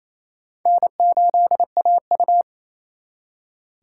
Morse code, heard as a keyed tone, about 34 wpm, 700 Hz. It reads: N8AU
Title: Speed Racing - USA 1x2 Call Signs 28wpm